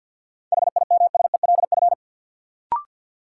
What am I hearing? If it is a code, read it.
Morse code, heard as a keyed tone, about 53 wpm, 700 Hz. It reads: HIMSELF
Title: Sentences from Top 1500 Words - New Word Drill - 50wpm